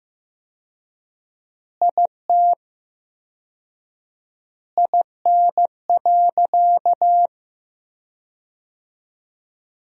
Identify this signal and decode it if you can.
Morse code, heard as a keyed tone, about 15 wpm, 700 Hz. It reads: IT IN.
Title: 4x Word-Spacing - Sentences from Top 200 Words - 15wpm